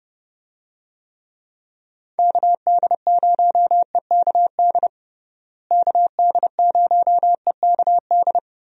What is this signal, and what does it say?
Morse code, heard as a keyed tone, about 30 wpm, 700 Hz. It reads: KD0EKD KD0EKD
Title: USA Call Signs - 6 Characters - Repeated 3x - 30wpm